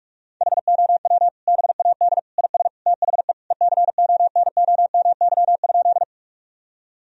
Morse code, heard as a keyed tone, about 46 wpm, 700 Hz. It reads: HOW BAD IS THE ECONOMY?
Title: Sentences from Top 1000 Words - Review Part 3 - 45wpm